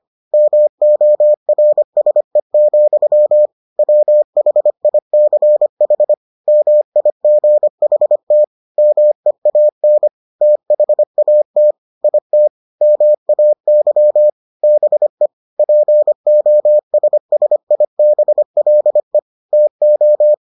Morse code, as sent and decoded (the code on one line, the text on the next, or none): -- --- .-. ... --..-- / .-- .... .. -.-. .... / -- .. --. .... - / -- . .- -. / - .... .- - / .. - / -- .- -.-- / -... / .--. --- ... ... .. -... .-.. / - ---
MORSE, WHICH MIGHT MEAN THAT IT MAY BE POSSIBLE TO